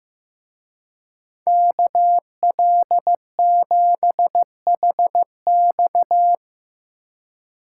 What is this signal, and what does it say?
Morse code, heard as a keyed tone, about 17 wpm, 700 Hz. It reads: KL7HX
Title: Speed Racing - USA 2x2 Call Signs 15wpm